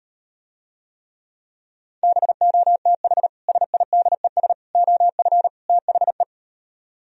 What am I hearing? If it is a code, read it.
Morse code, heard as a keyed tone, about 38 wpm, 700 Hz. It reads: BOTH SIDES OF THE